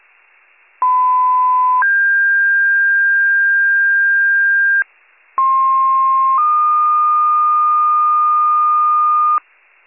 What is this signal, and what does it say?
Two-tone page, heard as a one-second tone, then a three-second tone, about 1010 then 1600 Hz, about 1060 then 1220 Hz, with noise. It is heard from 0.8 to 9.4 s.